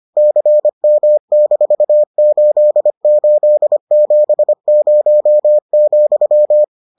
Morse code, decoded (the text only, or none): CM-8870,